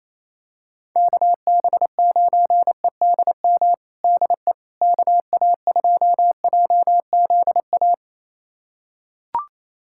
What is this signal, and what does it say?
Morse code, heard as a keyed tone, about 28 wpm, 700 Hz. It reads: KB9EDM DE KA2JZA
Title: USA Callsign DE Callsign 28wpm